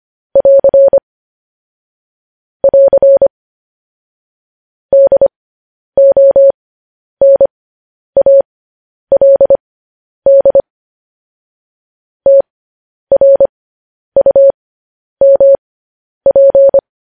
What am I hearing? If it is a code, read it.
Morse code, heard as a keyed tone, about 25 wpm, spaced as for 10 wpm, 550 Hz. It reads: <AR> <AR> DONALD TRUMP
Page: news headlines in morse code at 10 wpm